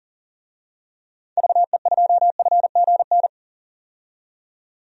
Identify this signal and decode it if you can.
Morse code, heard as a keyed tone, about 40 wpm, 700 Hz. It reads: VE2FGN
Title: DXCC-100 Call Signs - 6 Characters - 40wpm